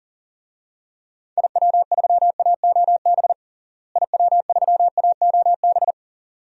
Morse code, heard as a keyed tone, about 40 wpm, 700 Hz. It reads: IW3AOB IW3AOB
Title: DXCC-100 Call Signs - 6 Characters - Repeated 2x - 40wpm